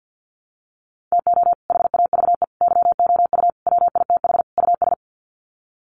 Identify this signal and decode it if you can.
Morse code, heard as a keyed tone, about 50 wpm, 700 Hz. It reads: TO HAVE YOU WITH US
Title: Sets of 5 Words 50wpm